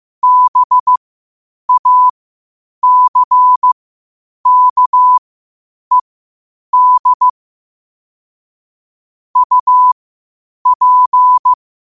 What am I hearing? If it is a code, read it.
Morse code, heard as a keyed tone, about 15 wpm, spaced as for 8 wpm, 1000 Hz. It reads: BACKED UP
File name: qotdfiles / 8wpm